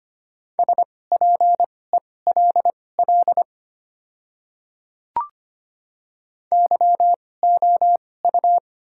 Morse code, heard as a keyed tone, about 25 wpm, spaced as for 18 wpm, 690 Hz. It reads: SPELL YOU
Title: 2x Character Spacing - Top 200 Words - 25wpm